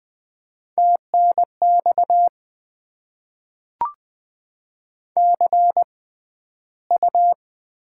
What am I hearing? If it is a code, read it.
Morse code, heard as a keyed tone, about 20 wpm, spaced as for 12 wpm, 700 Hz. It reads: TNX CU
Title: Top 20 QSO Elements - 6x Character Spacing - 20wpm